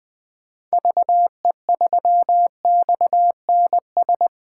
Morse code, heard as a keyed tone, about 20 wpm, 700 Hz. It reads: VE3XNS
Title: 2023 POTA Activator Call Signs - 20wpm